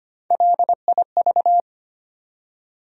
Morse code, heard as a keyed tone, about 25 wpm, 700 Hz. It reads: LIV
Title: USA Common Names - Max 4 Letters - 25wpm